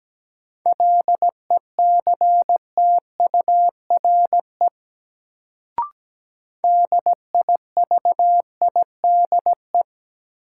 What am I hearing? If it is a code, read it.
Morse code, heard as a keyed tone, about 17 wpm, 700 Hz. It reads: LECTURE DIVIDE
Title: Sentences from Top 1600 Words - New Word Drill - 17wpm